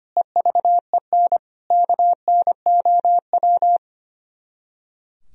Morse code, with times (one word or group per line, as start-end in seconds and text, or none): 0.2-1.4 EVEN
1.7-3.8 KNOW